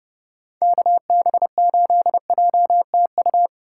Morse code, heard as a keyed tone, about 36 wpm, 700 Hz. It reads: KB8JTU
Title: Speed Racing - USA 2x3 Call Signs 30wpm